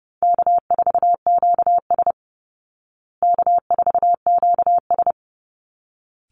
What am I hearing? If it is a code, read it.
Morse code, heard as a keyed tone, about 30 wpm, 700 Hz. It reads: K4QS K4QS
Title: Call Signs Repeated 3x - 30wpm